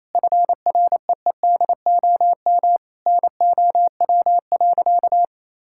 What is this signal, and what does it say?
Morse code, heard as a keyed tone, about 28 wpm, 700 Hz. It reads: FREEDOM NOW.
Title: Sentences from Top 1000 Words - Review Part 5 - 28wpm